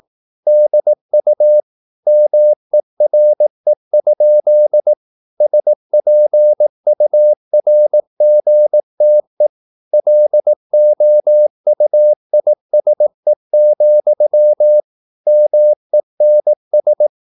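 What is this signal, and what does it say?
Morse code, heard as a keyed tone, about 18 wpm, 600 Hz. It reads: DU MERE? SPURGTE LOUISE, MENS
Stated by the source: Slow (QRS) Morse practice at 18 wpm, CW med Gnister.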